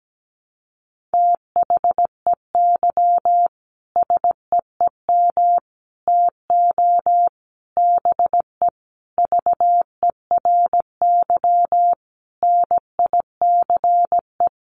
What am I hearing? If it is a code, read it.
Morse code, heard as a keyed tone, about 17 wpm, 700 Hz. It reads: THEY SEEM TO BE VERY NICE
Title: Sentences from Top 1100 Words 17wpm